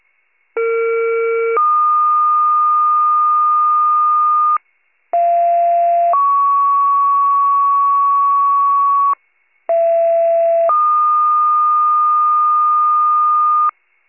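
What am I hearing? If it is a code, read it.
Two-tone page, heard as a one-second tone, then a three-second tone, about 460 then 1220 Hz, about 690 then 1120 Hz, about 670 then 1220 Hz. It is heard from 0.6 to 13.7 s.
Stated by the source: County-wide rescue paging archive, wcalert, channel 41_RESCUE.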